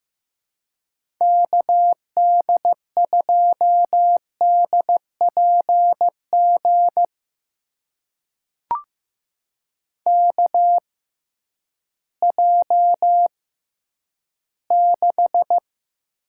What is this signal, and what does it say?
Morse code, heard as a keyed tone, about 15 wpm, spaced as for 10 wpm, 700 Hz. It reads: KD2DPG KJ6
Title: USA Call Signs - 6 Characters - 6x Character Spacing - 15wpm